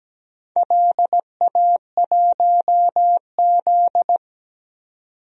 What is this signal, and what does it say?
Morse code, heard as a keyed tone, about 17 wpm, 700 Hz. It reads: LA1Z
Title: DXCC-100 Call Signs - 4 Characters - 17wpm